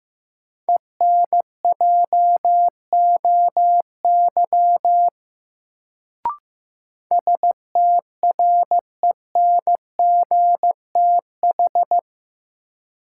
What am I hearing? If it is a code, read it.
Morse code, heard as a keyed tone, about 16 wpm, 700 Hz. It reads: ENJOY STRENGTH
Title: Sentences from Top 1300 Words - New Word Drill Alt - 15wpm